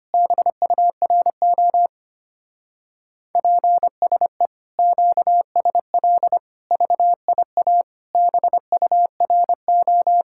Morse code, heard as a keyed tone, about 32 wpm, 700 Hz. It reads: BURO PSE QSL VIA BURO